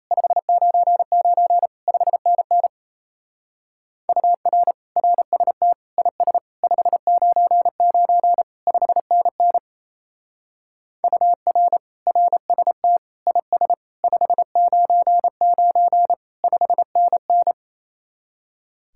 Morse code, heard as a keyed tone, about 33 wpm, 700 Hz. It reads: 599 5NN UR RST IS 599 5NN UR RST IS 599 5NN
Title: Speed-Racing Common QSO Phrases - 28wpm